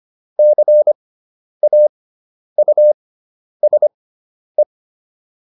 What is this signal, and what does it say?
Morse code, heard as a keyed tone, about 25 wpm, spaced as for 10 wpm, 600 Hz. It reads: CAUSE